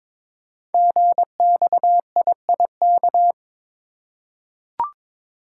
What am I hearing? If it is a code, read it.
Morse code, heard as a keyed tone, about 23 wpm, 700 Hz. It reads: GXIIK
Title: Code Groups - 5 Letters - 22wpm